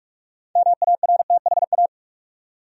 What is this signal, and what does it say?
Morse code, heard as a keyed tone, about 45 wpm, 700 Hz. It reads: MARTHA
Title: USA Common Names - 45wpm